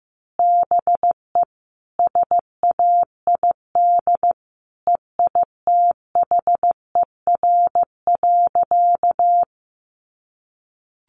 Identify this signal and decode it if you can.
Morse code, heard as a keyed tone, about 15 wpm, 700 Hz. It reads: BE SAID EITHER.
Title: Sentences from Top 600 Words 15wpm